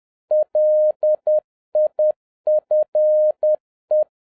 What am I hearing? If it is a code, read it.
Morse code, heard as a keyed tone, about 10 wpm, 610 Hz. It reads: LIFE